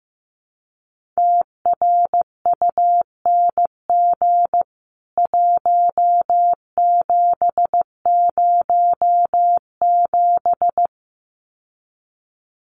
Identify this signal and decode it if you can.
Morse code, heard as a keyed tone, about 15 wpm, 700 Hz. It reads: TRUNG 1707
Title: CWT Exchange 15wpm